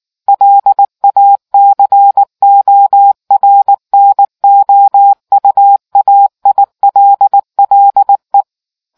Morse code, heard as a keyed tone, about 18 wpm, 800 Hz. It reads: LA CORNOUAILLE